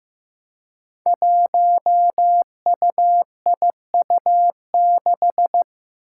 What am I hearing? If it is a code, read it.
Morse code, heard as a keyed tone, about 15 wpm, 700 Hz. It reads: 1UIU6
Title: Five Challenging Letter-Number Combinations 15wpm